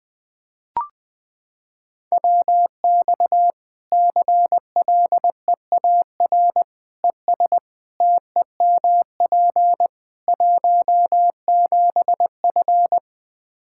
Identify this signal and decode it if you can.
Morse code, heard as a keyed tone, about 21 wpm, 700 Hz. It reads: WX CLEAR ES TEMP 17F
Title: Practice QSO - Exchange 2 - Info - 20wpm